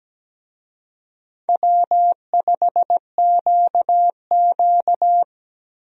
Morse code, heard as a keyed tone, about 17 wpm, 700 Hz. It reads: W5QQ
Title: USA Call Signs - 4 Characters - Repeated 3x - 17wpm